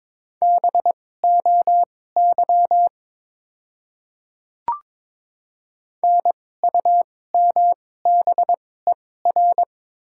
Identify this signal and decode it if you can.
Morse code, heard as a keyed tone, about 21 wpm, spaced as for 16 wpm, 700 Hz. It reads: BOY NUMBER
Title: 2x Character Spacing - Top 300 Words - 22wpm